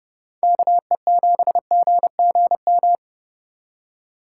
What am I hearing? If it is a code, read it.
Morse code, heard as a keyed tone, about 31 wpm, 700 Hz. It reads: KE7GGM